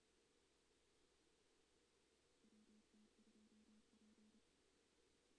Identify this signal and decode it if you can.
Morse code, heard as a keyed tone, about 29 wpm, 220 Hz. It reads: WT2P
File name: 2016 JAN CW NAQP / 2016 Jan naqp k5ot - 0-3